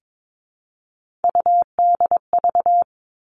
Morse code, heard as a keyed tone, about 22 wpm, 700 Hz. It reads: UDV
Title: All Cut Numbers - 3-Digit 22wpm